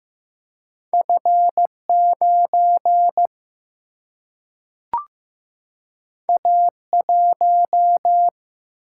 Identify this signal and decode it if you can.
Morse code, heard as a keyed tone, about 15 wpm, 700 Hz. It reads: F 9 A 1
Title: Single-Letter Single-Number 15wpm